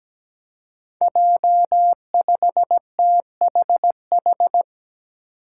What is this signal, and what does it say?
Morse code, heard as a keyed tone, about 17 wpm, 700 Hz. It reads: J5THH
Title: Five Challenging Letter-Number Combinations 17wpm